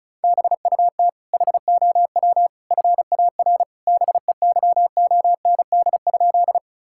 Morse code, heard as a keyed tone, about 36 wpm, 700 Hz. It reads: BUT HOW FAR BEYOND?
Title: Sentences from Top 1000 Words - Review Part 6 - 35wpm